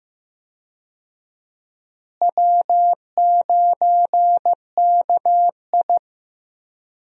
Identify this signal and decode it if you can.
Morse code, heard as a keyed tone, about 15 wpm, 700 Hz. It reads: W9KI
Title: USA Call Signs - 4 Characters - Repeated 3x - 15wpm